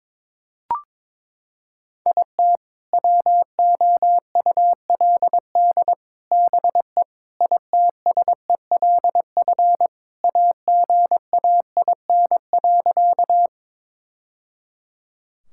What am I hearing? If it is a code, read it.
Morse code, heard as a keyed tone, about 21 wpm, 700 Hz. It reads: IT WOULD BE ITSELF AGAIN.
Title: Sentences from Top 1000 Words - Review Part 5 - 22wpm